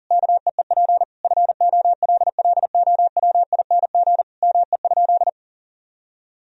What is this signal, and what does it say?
Morse code, heard as a keyed tone, about 40 wpm, 700 Hz. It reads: KEEP FOLLOWING ME?
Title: Sentences from Top 1000 Words - Review Part 4 - 40wpm